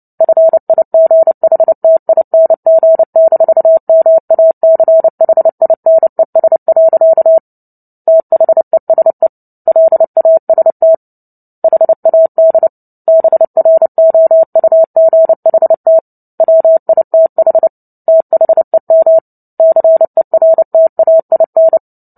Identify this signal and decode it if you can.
Morse code, heard as a keyed tone, about 30 wpm, 650 Hz. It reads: FIGHTING-MACHINES. THESE LAST HAD BROUGHT WITH THEM CERTAIN
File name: Book Two / 03-Chapter 3 - 29 WPM